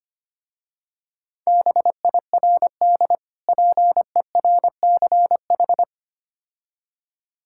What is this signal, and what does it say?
Morse code, heard as a keyed tone, about 25 wpm, 700 Hz. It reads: BIRD PERCH
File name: practice / 25wpm / Difficult Words with Hints - 25wpm